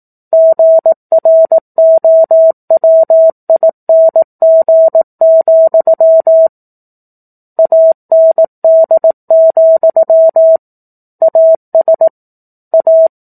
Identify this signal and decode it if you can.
Morse code, heard as a keyed tone, about 18 wpm, 650 Hz. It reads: GROWING, AND, AS A
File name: 04-Chapter-04 18 WPM